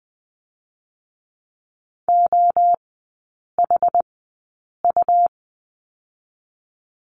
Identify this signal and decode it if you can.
Morse code, heard as a keyed tone, about 20 wpm, spaced as for 8 wpm, 700 Hz. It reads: OHU
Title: 145 Lesson 24e - Letter y - Three Characters 20wpm